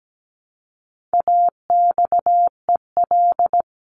Morse code, heard as a keyed tone, about 17 wpm, 700 Hz. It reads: AXEL
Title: CWT Exchange 17wpm